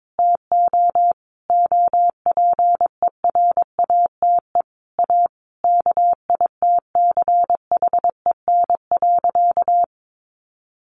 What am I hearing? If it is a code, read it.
Morse code, heard as a keyed tone, about 22 wpm, 700 Hz. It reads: TO OPERATE A KITCHEN.
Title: Sentences from Top 1500 Words 22wpm